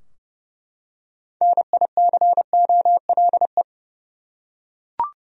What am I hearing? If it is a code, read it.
Morse code, heard as a keyed tone, about 30 wpm, 700 Hz. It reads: NICOLE
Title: USA Common Names - 30wpm